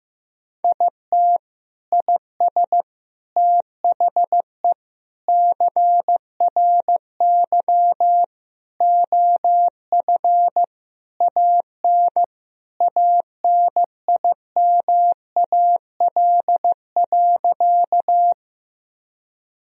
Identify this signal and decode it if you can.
Morse code, heard as a keyed tone, about 15 wpm, 700 Hz. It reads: IT IS THE CRY OF AN ANIMAL.
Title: Sentences from Top 1000 Words - Review Part 2 - 15wpm